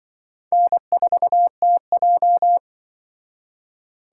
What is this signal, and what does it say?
Morse code, heard as a keyed tone, about 24 wpm, 700 Hz. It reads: N4TJ
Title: Speed Racing - USA Call Signs 20wpm